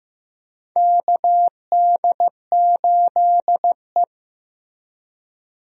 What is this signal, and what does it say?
Morse code, heard as a keyed tone, about 15 wpm, 700 Hz. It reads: KD8E
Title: USA Call Signs - 4 Characters - Repeated 2x - 15wpm